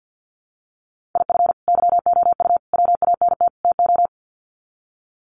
Morse code, heard as a keyed tone, about 50 wpm, 700 Hz. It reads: IF YOU WANT TO
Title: Sets of 4 Words 50wpm